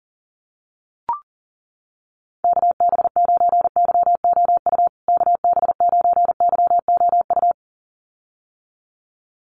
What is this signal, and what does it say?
Morse code, heard as a keyed tone, about 40 wpm, 700 Hz. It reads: KB9YOU KB9YOU
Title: USA Call Signs Repeated 2x 40wpm